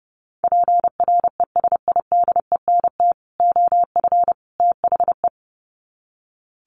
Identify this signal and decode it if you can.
Morse code, heard as a keyed tone, about 30 wpm, 700 Hz. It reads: PRESIDENT OF THE
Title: Sets of 3 Words 30wpm